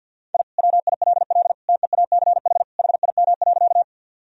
Morse code, heard as a keyed tone, about 50 wpm, 700 Hz. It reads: I WILL TEACH HIM.
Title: Sentences from Top 1000 Words - Review Part 2 - 50wpm